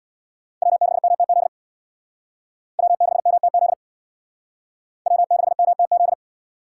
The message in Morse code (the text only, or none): W6MTZ W6MTZ W6MTZ